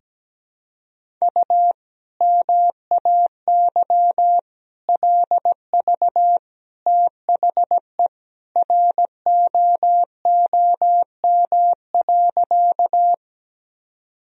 Morse code, heard as a keyed tone, about 17 wpm, 700 Hz. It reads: U MAY LV THE ROOM.
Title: Sentences from Top 400 Words and Abbreviations 17wpm